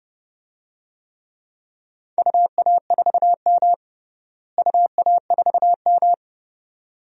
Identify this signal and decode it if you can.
Morse code, heard as a keyed tone, about 32 wpm, 700 Hz. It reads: UA4M UA4M
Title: DXCC-100 Call Signs - 4 Characters - Repeated 2x - 30wpm